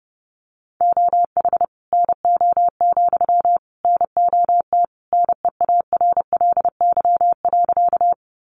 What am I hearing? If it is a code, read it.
Morse code, heard as a keyed tone, about 30 wpm, 700 Hz. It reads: OH NO, NOT NEARLY.
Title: Sentences from Top 1300 Words 30wpm